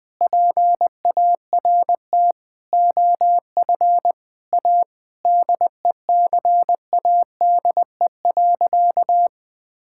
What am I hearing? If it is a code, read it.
Morse code, heard as a keyed tone, about 20 wpm, 700 Hz. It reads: PART OF A DECADE.
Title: Sentences with 6-Word Phrases - Spoken Beginning - 20wpm